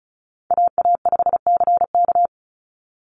Morse code, heard as a keyed tone, about 34 wpm, 700 Hz. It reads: AA5CK
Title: NA SOTA Chaser Callsigns 35wpm